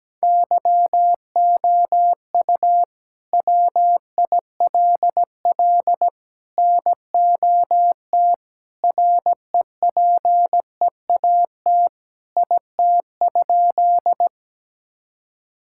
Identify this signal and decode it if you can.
Morse code, heard as a keyed tone, about 17 wpm, 700 Hz. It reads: YOU WILL NOT REPEAT IT?
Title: Sentences from Top 1600 Words 17wpm